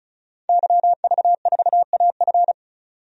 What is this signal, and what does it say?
Morse code, heard as a keyed tone, about 34 wpm, 700 Hz. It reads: YV4AF
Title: DXCC-100 Call Signs - 5 Characters - 35wpm